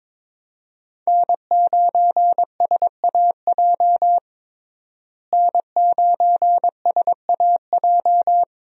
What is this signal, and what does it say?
Morse code, heard as a keyed tone, about 22 wpm, 700 Hz. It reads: N9SAJ N9SAJ
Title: USA Call Signs - 5 Characters - Repeated 3x - 22wpm